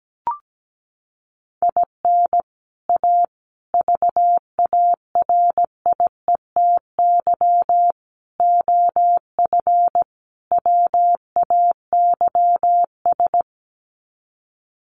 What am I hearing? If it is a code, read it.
Morse code, heard as a keyed tone, about 17 wpm, 700 Hz. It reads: IN A VARIETY OF WAYS
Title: Sets of 5 Words Encore 17wpm